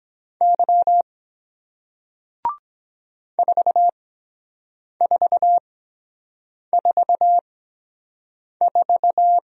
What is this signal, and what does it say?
Morse code, heard as a keyed tone, about 23 wpm, spaced as for 7 wpm, 700 Hz. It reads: Y 4444